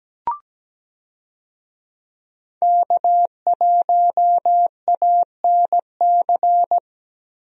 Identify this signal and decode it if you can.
Morse code, heard as a keyed tone, about 17 wpm, 700 Hz. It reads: K1ANC
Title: USA Call Signs - 5 Characters - Repeated 2x - 17wpm